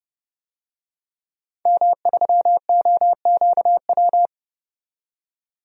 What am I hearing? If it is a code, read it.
Morse code, heard as a keyed tone, about 31 wpm, 700 Hz. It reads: M3OQW
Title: DXCC-100 Call Signs - 5 Characters - 30wpm